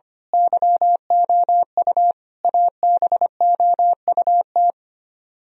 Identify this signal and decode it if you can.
Morse code, heard as a keyed tone, about 28 wpm, 700 Hz. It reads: YOU ABOUT
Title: Speed-Racing - Sets of 5 Words 25wpm